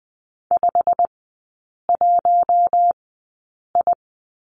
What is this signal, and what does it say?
Morse code, heard as a keyed tone, about 20 wpm, spaced as for 8 wpm, 700 Hz. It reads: 51I
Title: 159 Lesson 26e - Letter g - Three Characters 20wpm